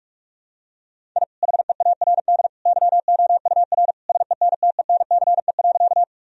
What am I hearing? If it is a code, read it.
Morse code, heard as a keyed tone, about 41 wpm, 700 Hz. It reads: I HEARD YOUR SENTENCE.
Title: Sentences from Top 1000 Words - Review Part 1 - 45wpm